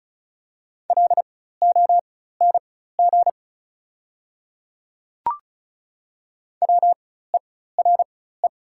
Morse code, heard as a keyed tone, about 36 wpm, spaced as for 16 wpm, 700 Hz. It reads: LONG WERE
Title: 4x Character Spacing - Top 200 Words - 35wpm